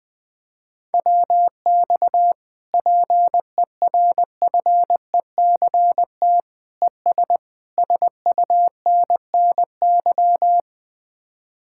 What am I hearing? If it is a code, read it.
Morse code, heard as a keyed tone, about 20 wpm, 700 Hz. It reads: WX PERFECT ES SUNNY
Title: Speed-Racing Common QSO Phrases - 17wpm